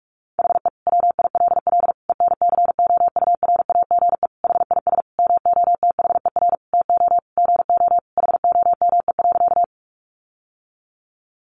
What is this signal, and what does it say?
Morse code, heard as a keyed tone, about 44 wpm, 700 Hz. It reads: HE WILL ENCOURAGE HIS MOTHER TO GO HOME.
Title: Sentences from Top 1400 Words 45wpm